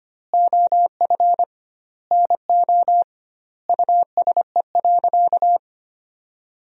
Morse code, heard as a keyed tone, about 25 wpm, 700 Hz. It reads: OF NO USE.
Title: 2x Word-Spacing - Sentences from Top 100 Words - 25wpm